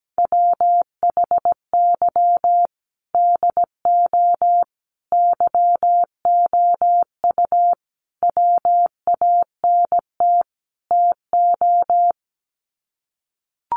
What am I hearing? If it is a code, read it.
Morse code, heard as a keyed tone, about 17 wpm, 700 Hz. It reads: WHY DO YOU WANT TO
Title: Sets of 5 Words Encore2 17wpm